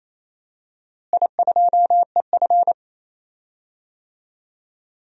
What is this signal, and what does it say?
Morse code, heard as a keyed tone, about 28 wpm, 700 Hz. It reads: I2EF
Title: DXCC-100 Call Signs - 4 Characters - Repeated 3x - 28wpm